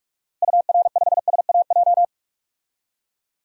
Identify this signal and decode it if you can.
Morse code, heard as a keyed tone, about 44 wpm, 700 Hz. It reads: UR5SAJ
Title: DXCC-100 Call Signs - 6 Characters - Repeated 3x - 45wpm